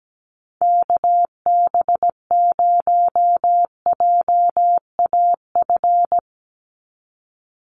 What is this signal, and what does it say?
Morse code, heard as a keyed tone, about 17 wpm, 700 Hz. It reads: KB0JAF
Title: USA Call Signs 17wpm